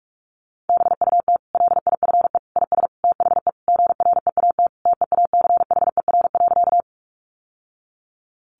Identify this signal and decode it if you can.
Morse code, heard as a keyed tone, about 44 wpm, 700 Hz. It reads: BUT LIFE IS THE GREAT TEACHER.
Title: Sentences from Top 900 Words 45wpm